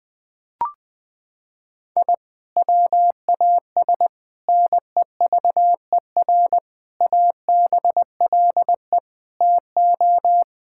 Morse code, heard as a keyed tone, about 20 wpm, 700 Hz. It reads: I WAS NEVER ABLE TO
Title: Sentences from Top 1000 Words - Review Part 6 - 20wpm